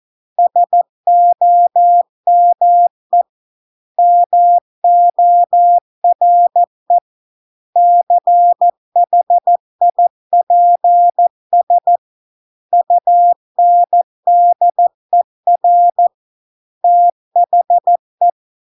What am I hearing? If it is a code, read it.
Morse code, heard as a keyed tone, about 14 wpm, 700 Hz. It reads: SOME MORE CHIPS UNDER THE